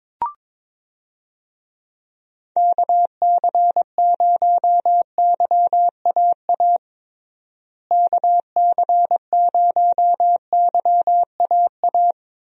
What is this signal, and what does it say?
Morse code, heard as a keyed tone, about 22 wpm, 700 Hz. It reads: KC0YAA KC0YAA
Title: USA Call Signs - 6 Characters - Repeated 3x - 22wpm